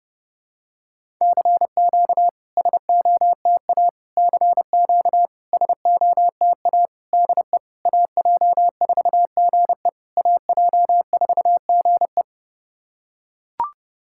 Morse code, heard as a keyed tone, about 30 wpm, 690 Hz. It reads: CQ SOTA CQ SOTA DE AJ4GE AJ4GE
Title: SOTA QSO Practice 30wpm